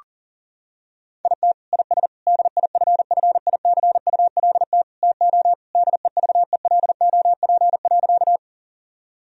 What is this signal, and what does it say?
Morse code, heard as a keyed tone, about 40 wpm, 700 Hz. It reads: IT IS DIFFICULT TO DEVELOP.
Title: Sentences from Top 1000 Words - Review Part 2 - 40wpm